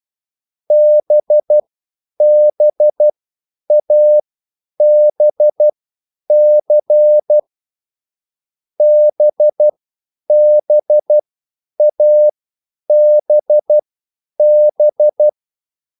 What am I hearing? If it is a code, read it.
Morse code, heard as a keyed tone, about 12 wpm, spaced as for 9 wpm, 600 Hz. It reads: BBABC BBABB